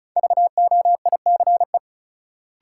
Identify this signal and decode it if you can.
Morse code, heard as a keyed tone, about 36 wpm, 700 Hz. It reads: VOICE
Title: Sentences from Top 500 Words - New Word Drill - 35wpm